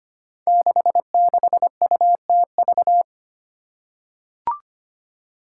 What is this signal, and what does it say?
Morse code, heard as a keyed tone, about 25 wpm, 700 Hz. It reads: 66UTV
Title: Five Challenging Letter-Number Combinations 25wpm